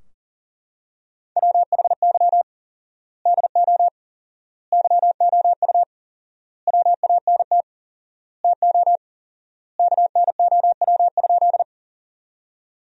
Morse code, heard as a keyed tone, about 40 wpm, 700 Hz. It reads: WHY DO YOU WANT TO KNOW?
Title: 4x Word-Spacing - Sentences from Top 200 Words - 40wpm